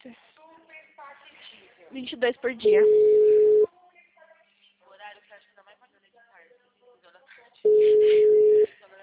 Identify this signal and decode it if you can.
Ringback tone (Europe, 425 Hz), with speech.
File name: out- -2011- - - .8